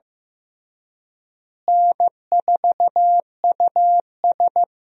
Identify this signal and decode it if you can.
Morse code, heard as a keyed tone, about 15 wpm, 700 Hz. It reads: N4US
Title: USA Call Signs - 4 Characters - Repeated 3x - 15wpm